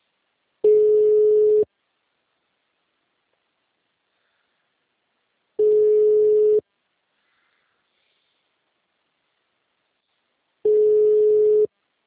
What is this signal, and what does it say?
Ringback tone (Europe, 425 Hz).